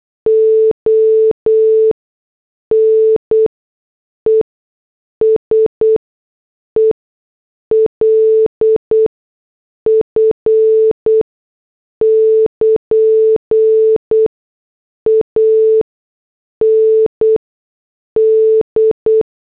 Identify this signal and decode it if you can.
Morse code, heard as a keyed tone, about 8 wpm, spaced as for 6 wpm, 440 Hz. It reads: ONESELF <KN> AND